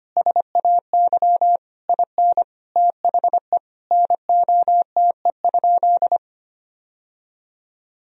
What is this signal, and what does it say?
Morse code, heard as a keyed tone, about 25 wpm, 700 Hz. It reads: SAY IN THE NOTE?
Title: Sentences from Top 1000 Words - Review Part 2 - 25wpm